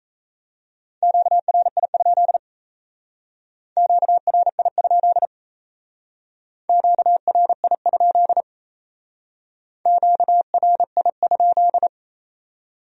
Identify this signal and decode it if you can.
Morse code, heard as a keyed tone, about 35 wpm, 700 Hz. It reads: QRI? QRI? QRI? QRI?
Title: Speed-Racing Q-Codes - 28wpm